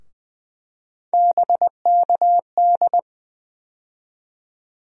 Morse code, heard as a keyed tone, about 20 wpm, 700 Hz. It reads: BKD
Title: USA Call Sign Postfixes - 3 Letters 20wpm